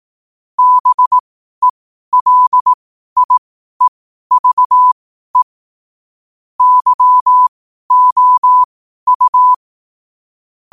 Morse code, heard as a keyed tone, about 18 wpm, spaced as for 12 wpm, 1000 Hz. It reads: BELIEVE YOU
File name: qotdfiles / 12wpm